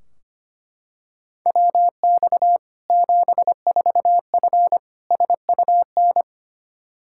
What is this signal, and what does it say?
Morse code, heard as a keyed tone, about 25 wpm, 700 Hz. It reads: WX 74F SUN
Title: Practice QSO - Exchange 2 - Info - 25wpm